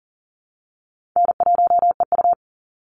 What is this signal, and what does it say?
Morse code, heard as a keyed tone, about 40 wpm, 700 Hz. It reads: N1EU
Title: NA SOTA Chaser Callsigns 40wpm